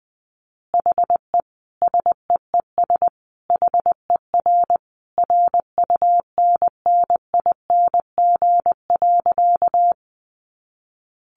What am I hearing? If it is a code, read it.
Morse code, heard as a keyed tone, about 20 wpm, 700 Hz. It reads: HE SEES HER RUNNING.
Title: Additional Easy Phrases 20wpm